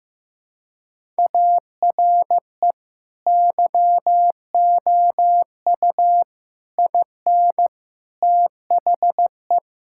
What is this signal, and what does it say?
Morse code, heard as a keyed tone, about 16 wpm, 700 Hz. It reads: ARE YOU IN THE
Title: Sentences from Top 1000 Words - Review Part 5 - 15wpm